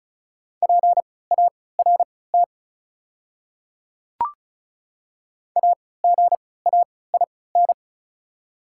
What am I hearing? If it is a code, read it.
Morse code, heard as a keyed tone, about 34 wpm, spaced as for 20 wpm, 700 Hz. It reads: PART AGAIN
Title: 3x Character Spacing - Top 300 Words - 35wpm